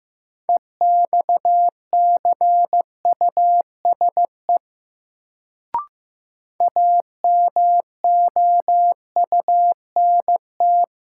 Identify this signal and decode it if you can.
Morse code, heard as a keyed tone, about 15 wpm, 700 Hz. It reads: EXCUSE AMOUNT
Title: Sentences from Top 700 Words - New Word Drill - 15wpm